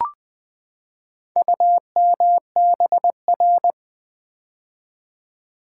Morse code, heard as a keyed tone, about 20 wpm, 700 Hz. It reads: UMBR
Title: Build A Word - 4 Letters At A Time - 8-Letter Words - 20wpm